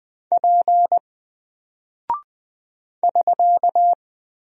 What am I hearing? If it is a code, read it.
Morse code, heard as a keyed tone, about 20 wpm, 700 Hz. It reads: P <SK>
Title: Single Character (Letter:Number:Punctuation:Prosign) Alt 20wpm